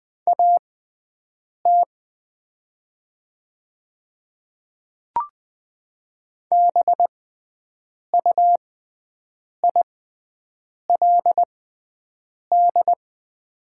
Morse code, heard as a keyed tone, about 20 wpm, spaced as for 7 wpm, 700 Hz. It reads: AT BUILD
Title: 6x Character Spacing - Top 300 Words - 20wpm